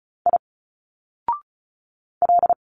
Morse code, heard as a keyed tone, about 35 wpm, 700 Hz. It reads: I L
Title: Single Letters 35wpm